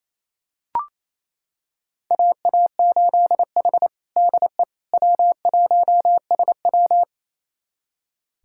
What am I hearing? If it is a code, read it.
Morse code, heard as a keyed tone, about 28 wpm, 700 Hz. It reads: AA8H DE W1SW
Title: USA Callsign DE Callsign - 4 Characters - 28wpm